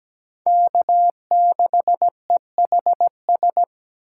Morse code, heard as a keyed tone, about 20 wpm, 700 Hz. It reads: K6EHS
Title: Speed Racing - USA 1x3 Call Signs 17wpm